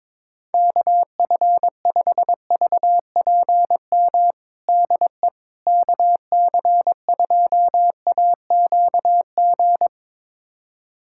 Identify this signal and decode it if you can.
Morse code, heard as a keyed tone, about 22 wpm, 700 Hz. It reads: KF5VPM DE KC2AQG
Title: USA Callsign DE Callsign - 6 Characters - 22wpm